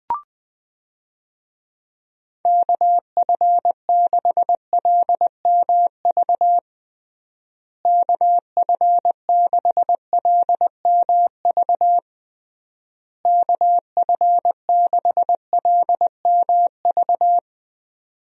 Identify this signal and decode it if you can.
Morse code, heard as a keyed tone, about 20 wpm, 700 Hz. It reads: KF6LMV KF6LMV KF6LMV
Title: USA Call Signs - 6 Characters - Repeated 3x - 20wpm